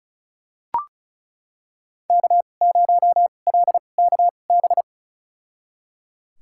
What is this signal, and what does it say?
Morse code, heard as a keyed tone, about 36 wpm, spaced as for 25 wpm, 700 Hz. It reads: K0LKB
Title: USA Call Signs - 5 Characters - 2x Character Spacing - 35wpm